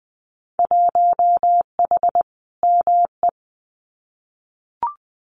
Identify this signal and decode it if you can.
Morse code, heard as a keyed tone, about 20 wpm, 700 Hz. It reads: 1H ME
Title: Winter Field Day Exchanges 20wpm